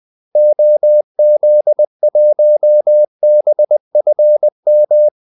Morse code, heard as a keyed tone, about 20 wpm, 600 Hz. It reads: OZ1BFM
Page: Kaldesignaler 20 wpm